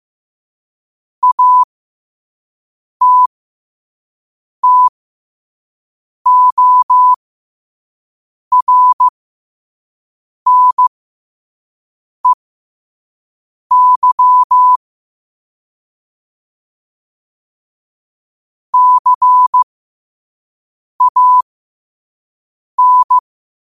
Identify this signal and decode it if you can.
Morse code, heard as a keyed tone, about 15 wpm, spaced as for 5 wpm, 1000 Hz. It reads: ATTORNEY CAN
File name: qotdfiles / 5wpm